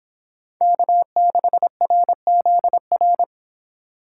Morse code, heard as a keyed tone, about 26 wpm, 700 Hz. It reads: K6RZR